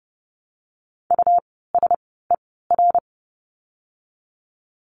Morse code, heard as a keyed tone, about 30 wpm, spaced as for 17 wpm, 700 Hz. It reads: USER
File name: Lesson 17f - Letter u - Words 30wpm